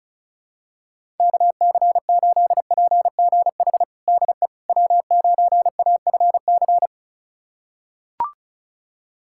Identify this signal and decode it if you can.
Morse code, heard as a keyed tone, about 36 wpm, 700 Hz. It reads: KC8PGH DE W9AFC
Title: USA Callsign DE Callsign 35wpm